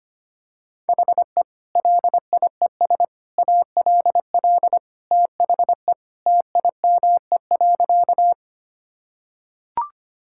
Morse code, heard as a keyed tone, about 25 wpm, 700 Hz. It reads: HE LIES ALL THE TIME.
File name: practice / 25wpm / Sentences with 3-Word Phrases - 25wpm